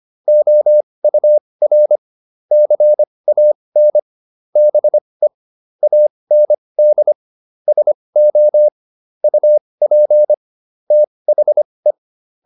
Morse code, heard as a keyed tone, about 25 wpm, spaced as for 20 wpm, 600 Hz. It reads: OUR CAN BE AND SO UP THE